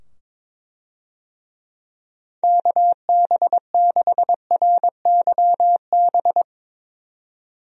Morse code, heard as a keyed tone, about 22 wpm, 700 Hz. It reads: KB6RYB
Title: USA Call Signs - 6 Characters - 22wpm